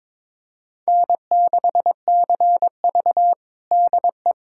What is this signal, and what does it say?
Morse code, heard as a keyed tone, about 22 wpm, 700 Hz. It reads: N6CV DE
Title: USA Callsign DE Callsign - 4 Characters - 22wpm